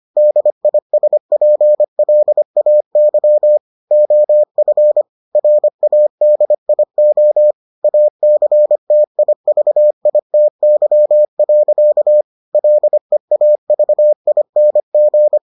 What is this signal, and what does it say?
Morse code, heard as a keyed tone, about 25 wpm, 600 Hz. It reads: DISPLAY OF RADIO ACTIVITY. LEAVING